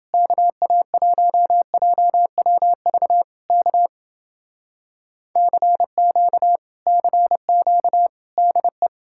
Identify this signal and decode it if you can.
Morse code, heard as a keyed tone, about 31 wpm, 700 Hz. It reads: KA1JWV K CQ CQ DE